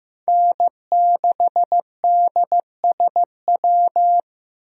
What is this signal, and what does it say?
Morse code, heard as a keyed tone, about 15 wpm, 700 Hz. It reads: N6DSW
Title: USA Call Signs - 5 Characters - Repeated 3x - 15wpm